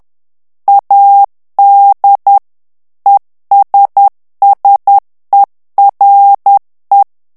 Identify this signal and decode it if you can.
Morse code, heard as a keyed tone, about 10 wpm, 800 Hz. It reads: AD ESSERE
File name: ITALIEN texte clair 2 vitesse 10 mots minute